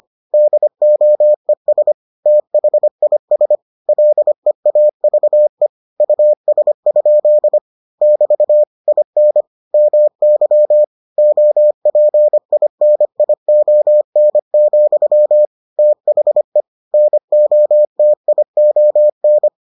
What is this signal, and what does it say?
Morse code, heard as a keyed tone, about 25 wpm, 600 Hz. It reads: DOES THIS LEAVE US? <BT> IN MY OPINION, THE NOTION